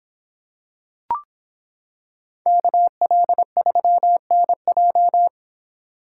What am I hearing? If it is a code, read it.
Morse code, heard as a keyed tone, about 25 wpm, 700 Hz. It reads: KL3NJ